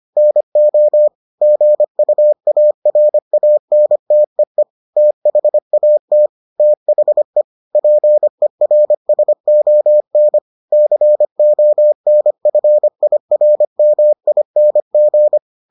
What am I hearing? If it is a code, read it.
Morse code, heard as a keyed tone, about 25 wpm, 600 Hz. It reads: NO GUARANTEE THAT THE PERSON CONFIRMING